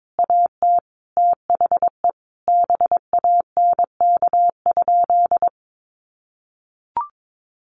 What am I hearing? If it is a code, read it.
Morse code, heard as a keyed tone, about 22 wpm, 700 Hz. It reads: AT THE BANK?
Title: Sentences from Top 1300 Words 22wpm